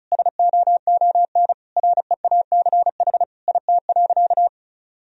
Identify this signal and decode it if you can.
Morse code, heard as a keyed tone, about 36 wpm, 700 Hz. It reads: SOON REACH IT.
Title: Sentences from Top 400 Words and Abbreviations 35wpm